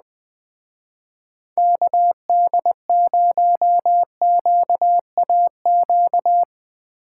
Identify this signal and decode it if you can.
Morse code, heard as a keyed tone, about 20 wpm, 700 Hz. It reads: KD0QAQ
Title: Speed Racing - USA 2x3 Call Signs 17wpm